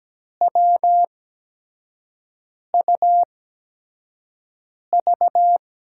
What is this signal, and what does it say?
Morse code, heard as a keyed tone, about 17 wpm, spaced as for 5 wpm, 700 Hz. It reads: WUV
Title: USA Call Signs - 8x Character Spacing - 17wpm